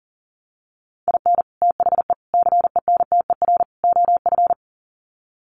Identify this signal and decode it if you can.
Morse code, heard as a keyed tone, about 40 wpm, 700 Hz. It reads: IN THE CENTER OF
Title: Sets of 4 Words 40wpm